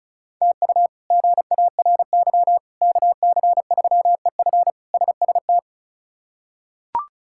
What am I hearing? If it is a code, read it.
Morse code, heard as a keyed tone, about 36 wpm, 700 Hz. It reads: TU GARY KC3EF SST
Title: K1USN SST QSO Practice - 35wpm